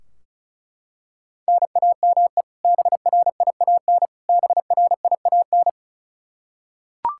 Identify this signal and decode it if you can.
Morse code, heard as a keyed tone, about 34 wpm, 700 Hz. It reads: NAME BRIAN BRIAN
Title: K1USN SST QSO Practice 35wpm